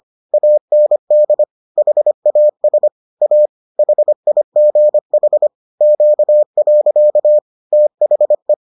Morse code, heard as a keyed tone, about 25 wpm, 600 Hz. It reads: AND HAS A HIGH Q. THE